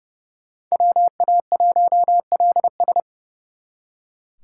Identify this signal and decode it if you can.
Morse code, heard as a keyed tone, about 38 wpm, 700 Hz. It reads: WA1LS